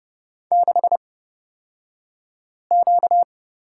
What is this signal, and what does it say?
Morse code, heard as a keyed tone, about 30 wpm, spaced as for 5 wpm, 700 Hz. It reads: 6Q
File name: Lesson 38d - Letter q - Two Characters 30wpm